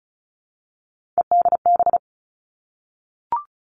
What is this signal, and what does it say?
Morse code, heard as a keyed tone, about 36 wpm, 700 Hz. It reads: EDB
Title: Three Challenging Letter-Number Combinations 35wpm